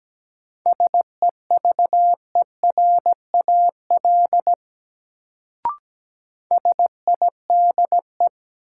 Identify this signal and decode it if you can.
Morse code, heard as a keyed tone, about 17 wpm, 700 Hz. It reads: SEVERAL SIDE
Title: Sentences from Top 400 Words - New Word Drill - 17wpm